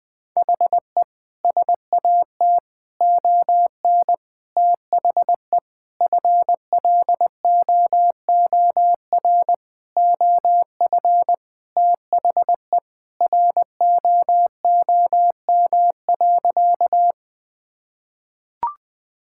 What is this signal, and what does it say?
Morse code, heard as a keyed tone, about 20 wpm, 700 Hz. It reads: HE SAT ON THE FLOOR OF THE ROOM.
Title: Sentences with 5-Word Phrases - 20wpm